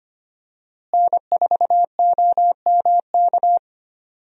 Morse code, heard as a keyed tone, about 25 wpm, 700 Hz. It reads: N4OMK
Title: USA Call Signs - 5 Characters - Repeated 3x - 25wpm